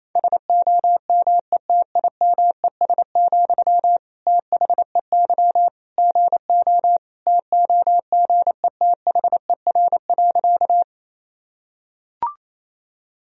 Morse code, heard as a keyed tone, about 28 wpm, 700 Hz. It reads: SOMETIMES, THEY GO TOGETHER.
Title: Sentences from Top 1000 Words - Review Part 3 - 28wpm